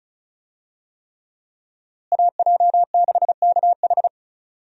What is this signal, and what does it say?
Morse code, heard as a keyed tone, about 36 wpm, 700 Hz. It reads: AJ6KH
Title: USA Call Signs - 5 Characters - Repeated 3x - 35wpm